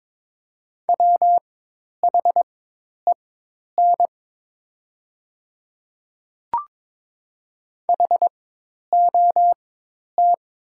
Morse code, heard as a keyed tone, about 22 wpm, spaced as for 10 wpm, 700 Hz. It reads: WHEN HOT
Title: 4x Character Spacing - Top 100 Words - 22wpm